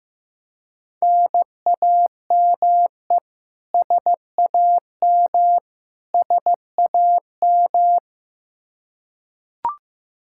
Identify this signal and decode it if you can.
Morse code, heard as a keyed tone, about 15 wpm, 700 Hz. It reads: NAME SAM SAM
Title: Practice QSO - Exchange 1 - Beginning - 15wpm